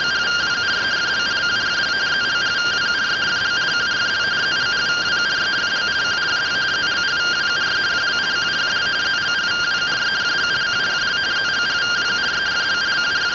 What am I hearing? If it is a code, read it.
Two-tone teleprinter signal, about 170 Hz shift.
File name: findings / CT AXIAL -256.gen-26 rtty rx